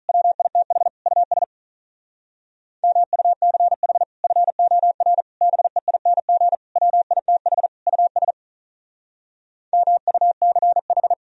Morse code, heard as a keyed tone, about 44 wpm, 700 Hz. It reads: WITH US MUCH FOR BEING WITH US MUCH
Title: Speed-Racing - Sets of 5 Words 35wpm